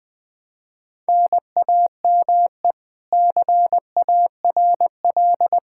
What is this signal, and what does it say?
Morse code, heard as a keyed tone, about 20 wpm, 700 Hz. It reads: NAME CARL
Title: K1USN SST QSO Practice 20wpm